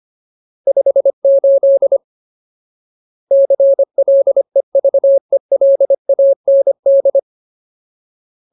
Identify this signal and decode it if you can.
Morse code, heard as a keyed tone, about 25 wpm, 550 Hz. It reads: 58 CLEVELAND
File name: Ultimate Word List / 25WPM Ultimate